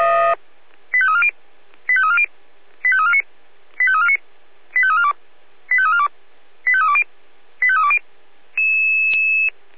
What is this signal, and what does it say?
Five-tone selective call (ZVEI), heard as quick runs of five tones, tones of about 70 ms. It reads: 85340 85340 85349 85349 85342 85342 85320 85320